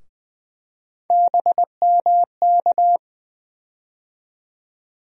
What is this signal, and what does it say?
Morse code, heard as a keyed tone, about 20 wpm, 700 Hz. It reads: BMK
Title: USA Call Sign Postfixes - 3 Letters 20wpm